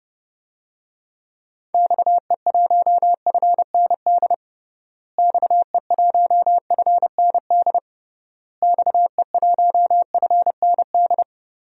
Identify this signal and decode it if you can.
Morse code, heard as a keyed tone, about 30 wpm, 700 Hz. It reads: XE1FND XE1FND XE1FND
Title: DXCC-100 Call Signs - 6 Characters - Repeated 3x - 30wpm